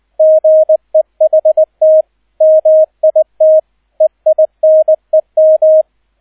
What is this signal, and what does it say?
Morse code, heard as a keyed tone, about 20 wpm, 630 Hz, with radio noise. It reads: GEHT MIT EINEM
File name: 002 Zweiter Streich 20wpm